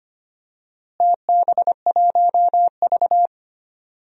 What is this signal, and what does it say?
Morse code, heard as a keyed tone, about 25 wpm, 700 Hz. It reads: TB1V
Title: DXCC-100 Call Signs - 4 Characters - Repeated 2x - 25wpm